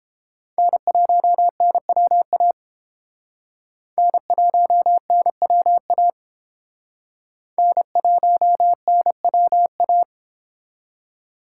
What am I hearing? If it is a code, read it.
Morse code, heard as a keyed tone, about 29 wpm, 700 Hz. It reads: N1NWA N1NWA N1NWA